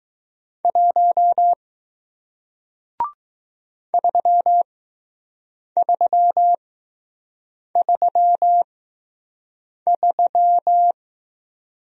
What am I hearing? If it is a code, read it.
Morse code, heard as a keyed tone, about 20 wpm, spaced as for 6 wpm, 700 Hz. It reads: 1 3333